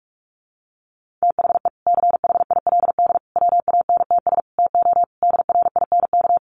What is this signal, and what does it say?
Morse code, heard as a keyed tone, about 44 wpm, 700 Hz. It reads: THE CHILD WANTS TO DRINK
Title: Additional Easy Phrases 45wpm